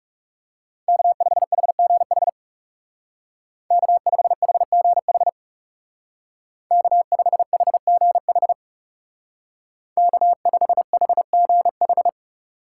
Morse code, heard as a keyed tone, about 35 wpm, 700 Hz. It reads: K5HGH K5HGH K5HGH K5HGH
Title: Speed Racing - USA Call Signs 30wpm